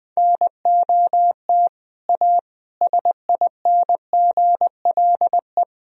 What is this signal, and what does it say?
Morse code, heard as a keyed tone, about 20 wpm, 700 Hz. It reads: NOT A SINGLE
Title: Sentences from Top 1000 Words - Review Part 4 - 20wpm